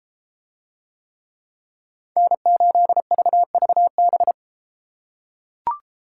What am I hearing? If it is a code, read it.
Morse code, heard as a keyed tone, about 31 wpm, 700 Hz. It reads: N8VVB